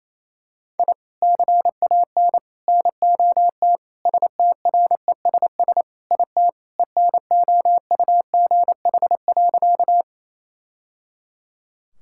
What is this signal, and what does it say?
Morse code, heard as a keyed tone, about 28 wpm, 700 Hz. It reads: I CAN NOT STRESS IT ENOUGH.
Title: Sentences from Top 1000 Words - Review Part 6 - 28wpm